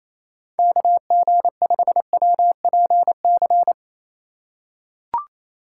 Morse code, heard as a keyed tone, about 28 wpm, 700 Hz. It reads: KG5WPC
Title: USA Call Signs - 4x Character Spacing - 28wpm